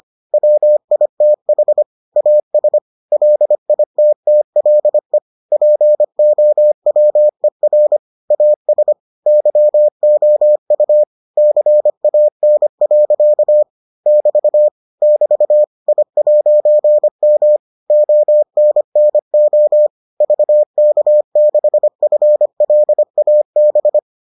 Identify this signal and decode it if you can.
Morse code, heard as a keyed tone, about 25 wpm, 600 Hz. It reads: WITH AS LITTLE POWER AS YOU CAN. <BT> <BT> I'M ONNO VK6FLAB